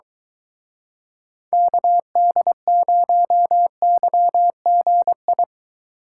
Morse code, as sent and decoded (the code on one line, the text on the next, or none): -.- -.. ----- -.-- --. ..
KD0YGI